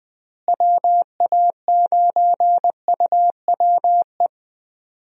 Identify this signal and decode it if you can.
Morse code, heard as a keyed tone, about 20 wpm, 700 Hz. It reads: WA9UWE
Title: USA Call Signs - 6 Characters - Repeated 3x - 20wpm